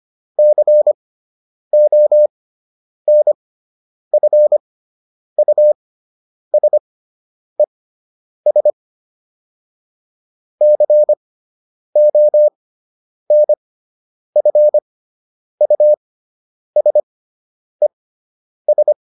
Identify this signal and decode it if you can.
Morse code, heard as a keyed tone, about 25 wpm, spaced as for 9 wpm, 600 Hz. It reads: CONFUSES CONFUSES